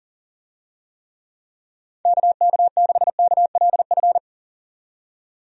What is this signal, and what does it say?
Morse code, heard as a keyed tone, about 40 wpm, 700 Hz. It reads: KK6KLF
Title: USA Call Signs - 6 Characters - 40wpm